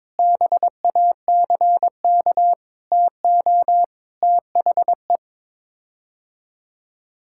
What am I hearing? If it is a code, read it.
Morse code, heard as a keyed tone, about 22 wpm, 700 Hz. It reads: BACK TO THE
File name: practice / 22wpm / Sets of 6 Words - 22wpm